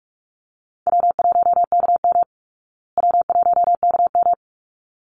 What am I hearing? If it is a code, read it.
Morse code, heard as a keyed tone, about 44 wpm, 700 Hz. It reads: W1KM W1KM
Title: Easy Call Signs Repeated 2x - 45wpm